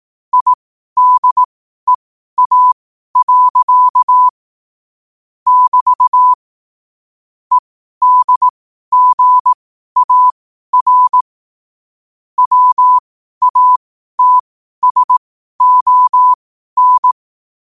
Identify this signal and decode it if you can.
Morse code, heard as a keyed tone, about 18 wpm, spaced as for 12 wpm, 1000 Hz. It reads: IDEA. <BT> EDGAR WATSON
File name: qotdfiles / 12wpm